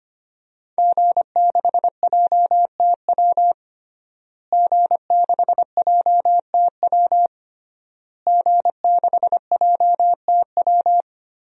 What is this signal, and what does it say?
Morse code, heard as a keyed tone, about 25 wpm, 700 Hz. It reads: G6JTW G6JTW G6JTW
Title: DXCC-100 Call Signs - 5 Characters - Repeated 3x - 25wpm